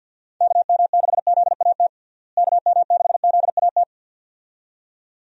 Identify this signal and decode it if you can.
Morse code, heard as a keyed tone, about 53 wpm, 700 Hz. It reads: KM6ZAT KM6ZAT
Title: USA Call Signs - 6 Characters - Repeated 2x - 50wpm